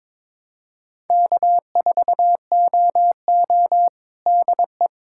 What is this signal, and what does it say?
Morse code, heard as a keyed tone, about 22 wpm, 700 Hz. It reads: K4OO DE
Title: USA Callsign DE Callsign - 4 Characters - 22wpm